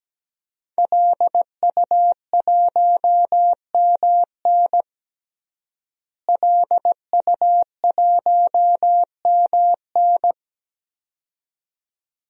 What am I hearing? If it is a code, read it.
Morse code, heard as a keyed tone, about 17 wpm, 700 Hz. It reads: LU1MN LU1MN
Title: DXCC-100 Call Signs - 5 Characters - Repeated 2x - 17wpm